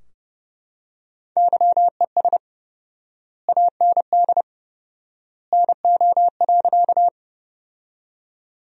Morse code, heard as a keyed tone, about 29 wpm, 700 Hz. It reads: YES AND NO.
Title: 4x Word-Spacing - Sentences from Top 500 Words - 30wpm